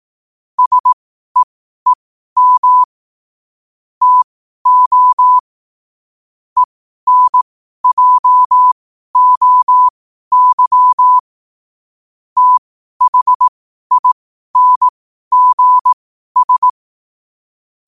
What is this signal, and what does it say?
Morse code, heard as a keyed tone, about 18 wpm, spaced as for 12 wpm, 1000 Hz. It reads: SEEM TO ENJOY THINGS
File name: qotdfiles / 12wpm